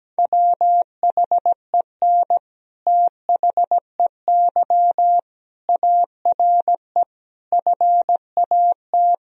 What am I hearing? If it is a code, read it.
Morse code, heard as a keyed tone, about 17 wpm, 700 Hz. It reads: WHEN THEY ARE FAT
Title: Sentences from Top 1000 Words - Review Part 4 - 17wpm